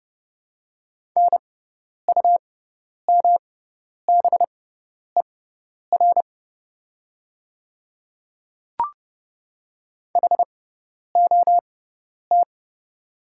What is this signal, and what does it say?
Morse code, heard as a keyed tone, about 31 wpm, spaced as for 10 wpm, 700 Hz. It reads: NUMBER HOT
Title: 6x Character Spacing - Top 100 Words - 30wpm